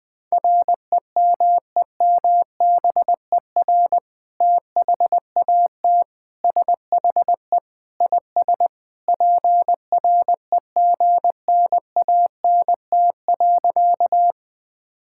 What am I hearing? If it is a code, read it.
Morse code, heard as a keyed tone, about 21 wpm, 700 Hz. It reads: REMEMBER THAT SHE IS PREGNANT.
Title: Sentences from Top 1000 Words - Review Part 6 - 20wpm